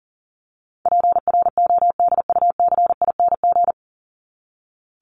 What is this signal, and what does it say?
Morse code, heard as a keyed tone, about 40 wpm, 700 Hz. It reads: PRODUCING
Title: ING Suffix - Top 100 Words with Base Words 40wpm